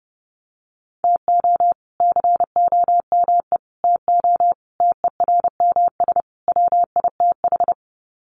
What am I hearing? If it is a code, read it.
Morse code, heard as a keyed tone, about 30 wpm, 700 Hz. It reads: TO COME TO TERMS WITH
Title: Sets of 5 Words Encore 30wpm